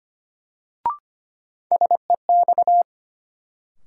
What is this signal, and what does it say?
Morse code, heard as a keyed tone, about 25 wpm, 700 Hz. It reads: SEX
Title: Sentences from Top 1600 Words - New Word Drill - 25wpm